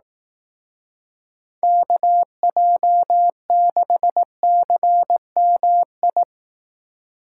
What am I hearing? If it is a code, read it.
Morse code, heard as a keyed tone, about 18 wpm, 700 Hz. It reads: KJ6CMI
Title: Speed Racing - USA Call Signs 15wpm